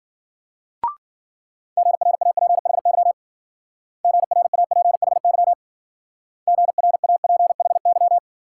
Morse code, heard as a keyed tone, about 52 wpm, 700 Hz. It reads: GRAPHY GRAPHY GRAPHY